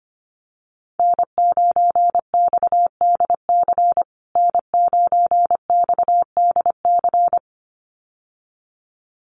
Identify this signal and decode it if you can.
Morse code, heard as a keyed tone, about 25 wpm, 700 Hz. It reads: N9XDC N9XDC
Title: USA Call Signs Repeated 2x 25wpm